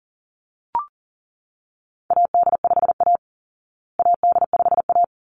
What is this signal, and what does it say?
Morse code, heard as a keyed tone, about 40 wpm, 700 Hz. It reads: AD5A AD5A
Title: Call Signs Repeated 3x - 40wpm